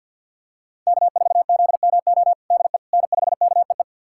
Morse code, heard as a keyed tone, about 46 wpm, 700 Hz. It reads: K4ZMO DE N5KEE
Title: USA Callsign DE Callsign - 5 Characters - 50wpm